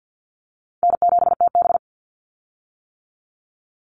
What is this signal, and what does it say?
Morse code, heard as a keyed tone, about 50 wpm, 700 Hz. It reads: N7TB
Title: Call Signs - 50wpm